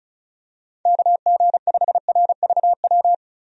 Morse code, heard as a keyed tone, about 36 wpm, 700 Hz. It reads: KG5RVW
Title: USA Call Signs - 6 Characters - 35wpm